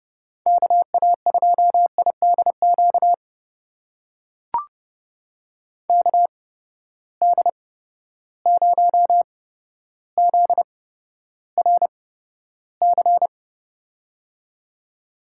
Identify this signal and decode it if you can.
Morse code, heard as a keyed tone, about 30 wpm, spaced as for 13 wpm, 700 Hz. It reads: KA2IDQ KD0ZRC
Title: USA Call Signs - 6 Characters - 8x Character Spacing - 30wpm